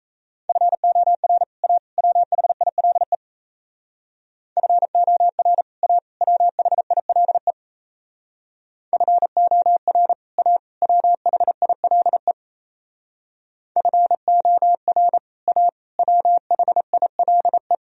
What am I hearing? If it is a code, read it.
Morse code, heard as a keyed tone, about 36 wpm, 700 Hz. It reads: FOR A WHILE FOR A WHILE FOR A WHILE FOR A WHILE